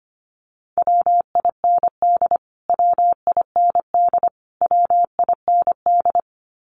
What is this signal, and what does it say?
Morse code, heard as a keyed tone, about 25 wpm, 700 Hz. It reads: WIND WIND WIND
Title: Top 500 Words Repeated 25wpm